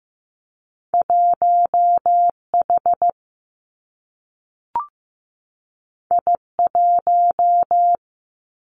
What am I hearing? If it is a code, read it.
Morse code, heard as a keyed tone, about 15 wpm, 700 Hz. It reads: 1 H I 1
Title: Two Challenging Letter-Number Combinations 15wpm